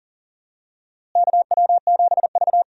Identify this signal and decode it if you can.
Morse code, heard as a keyed tone, about 40 wpm, 700 Hz. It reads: KW7V
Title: USA Call Signs - 4 Characters - Repeated 2x - 40wpm